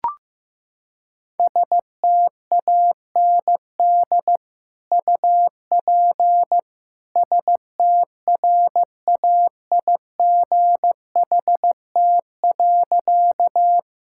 Morse code, heard as a keyed tone, about 15 wpm, 700 Hz. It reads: STAND UP STRAIGHT.
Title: Sentences from Top 1000 Words - Review Part 6 - 15wpm